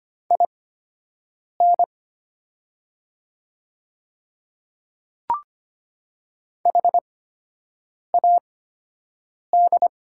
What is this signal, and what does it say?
Morse code, heard as a keyed tone, about 25 wpm, spaced as for 7 wpm, 700 Hz. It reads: IN HAD
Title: 8x Character Spacing - Top 100 Words - 25wpm